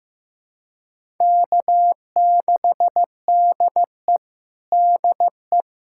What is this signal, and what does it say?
Morse code, heard as a keyed tone, about 15 wpm, 700 Hz. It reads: K6DE DE
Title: USA Callsign DE Callsign - 4 Characters - 15wpm